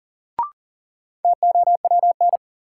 Morse code, heard as a keyed tone, about 40 wpm, 700 Hz. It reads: TOWN
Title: Sentences from Top 500 Words - New Word Drill - 40wpm